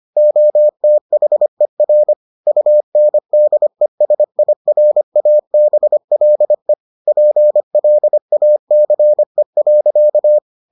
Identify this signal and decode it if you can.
Morse code, heard as a keyed tone, about 25 wpm, 600 Hz. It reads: OTHER UNDESIRABLE PLACE.